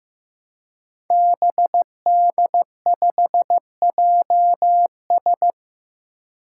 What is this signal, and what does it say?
Morse code, heard as a keyed tone, about 15 wpm, 700 Hz. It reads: BD5JS
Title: Five Challenging Letter-Number Combinations 15wpm